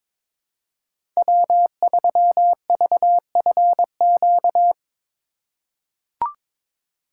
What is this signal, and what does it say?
Morse code, heard as a keyed tone, about 22 wpm, 700 Hz. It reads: W3VFQ
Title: USA Call Signs - 5 Characters - 2x Character Spacing - 22wpm